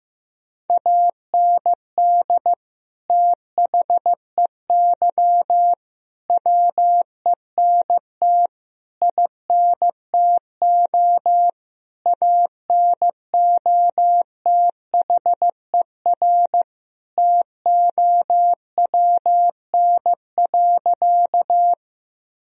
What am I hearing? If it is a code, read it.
Morse code, heard as a keyed tone, about 15 wpm, 700 Hz. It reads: AND THEY WENT INTO ANOTHER TOWN.
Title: Sentences from Top 1000 Words - Review Part 2 - 15wpm